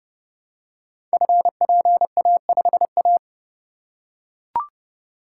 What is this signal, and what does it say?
Morse code, heard as a keyed tone, about 30 wpm, 700 Hz. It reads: FPA5A
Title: Code Groups - 5 Letter:Numbers - 30wpm